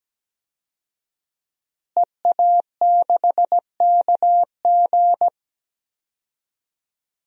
Morse code, heard as a keyed tone, about 17 wpm, 700 Hz. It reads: EA6KG
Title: DXCC-100 Call Signs - 5 Characters - Repeated 2x - 17wpm